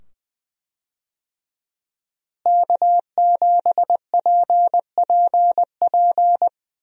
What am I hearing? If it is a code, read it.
Morse code, heard as a keyed tone, about 20 wpm, 700 Hz. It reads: K7PPP
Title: USA Call Signs - 5 Characters - Repeated 2x - 20wpm